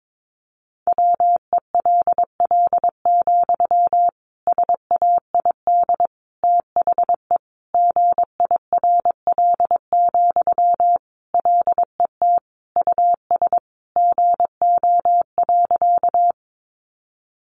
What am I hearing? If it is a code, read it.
Morse code, heard as a keyed tone, about 22 wpm, 700 Hz. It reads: WELL, SAID THE GIRL, LET US GO.
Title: Sentences from Top 400 Words 22wpm